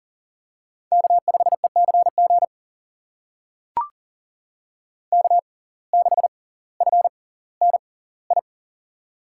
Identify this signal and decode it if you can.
Morse code, heard as a keyed tone, about 39 wpm, spaced as for 20 wpm, 700 Hz. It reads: K5ECG K6FNI
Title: USA Call Signs - 5 Characters - 6x Character Spacing - 40wpm